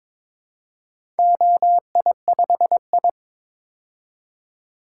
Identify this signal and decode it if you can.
Morse code, heard as a keyed tone, about 22 wpm, 700 Hz. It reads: OI5I
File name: practice / 22wpm / Four Letter-Number Combinations - 22wpm